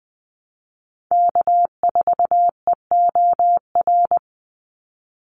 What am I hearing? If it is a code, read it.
Morse code, heard as a keyed tone, about 20 wpm, 700 Hz. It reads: K4EOR
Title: Call Signs - 20wpm